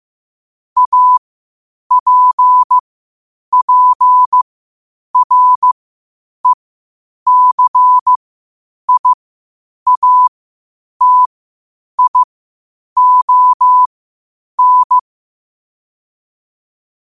Morse code, heard as a keyed tone, about 15 wpm, spaced as for 8 wpm, 1000 Hz. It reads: APPRECIATION
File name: qotdfiles / 8wpm